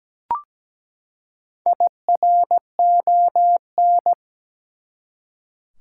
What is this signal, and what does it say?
Morse code, heard as a keyed tone, about 17 wpm, 700 Hz. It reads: IRON
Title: Four Letter Words - 17wpm